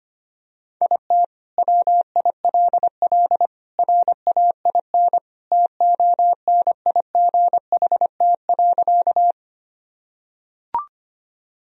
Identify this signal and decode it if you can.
Morse code, heard as a keyed tone, about 25 wpm, 700 Hz. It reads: IT WILL RAIN TONIGHT.
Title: Sentences from Top 1000 Words - Review Part 1 - 25wpm